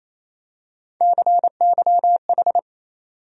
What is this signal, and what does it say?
Morse code, heard as a keyed tone, about 28 wpm, 700 Hz. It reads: CYH
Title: USA Call Sign Postfixes - 3 Letters 28wpm